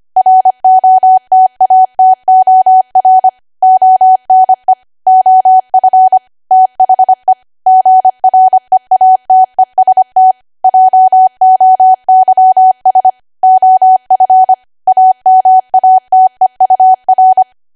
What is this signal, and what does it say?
Morse code, heard as a keyed tone, about 25 wpm, 750 Hz. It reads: ROTATOR ONE OF THE GREATEST JOYS OF AMATEUR
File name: Archive / 25 WPM / 25WPM